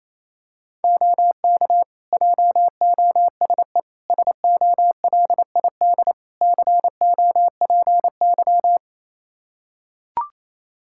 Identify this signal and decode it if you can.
Morse code, heard as a keyed tone, about 28 wpm, 700 Hz. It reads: OK JOSE SOLID COPY
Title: Practice QSO - Exchange 3 - More Info - 28wpm